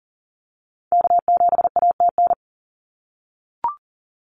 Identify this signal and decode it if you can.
Morse code, heard as a keyed tone, about 42 wpm, 700 Hz. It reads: K7ATN
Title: NA SOTA Activator Callsigns 40wpm